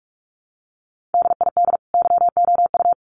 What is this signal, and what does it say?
Morse code, heard as a keyed tone, about 44 wpm, 700 Hz. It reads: DID YOU
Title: Sets of 2 Words 45wpm